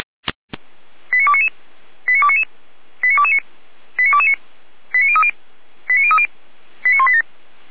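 Five-tone selective call (ZVEI), heard as quick runs of five tones, tones of about 70 ms. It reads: 89299 89299 89229 89229 89030 89030 89187
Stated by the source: ZVEI five-tone call 89173 received by SDR, archived by fire brigade Feuerwehr Orlamünde.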